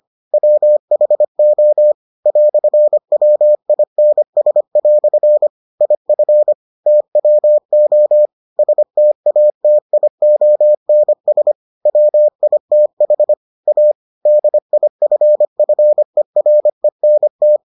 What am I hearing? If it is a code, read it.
Morse code, heard as a keyed tone, about 25 wpm, 600 Hz. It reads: WHO "WINS" IF TWO STATIONS WITH A DIFFERENT